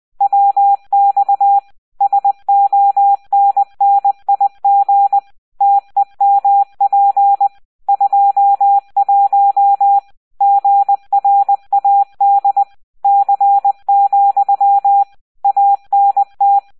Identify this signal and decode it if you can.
Morse code, heard as a keyed tone, about 20 wpm, 800 Hz, with radio noise. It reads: WX SONNIG TEMP 21 GRAD C, ANT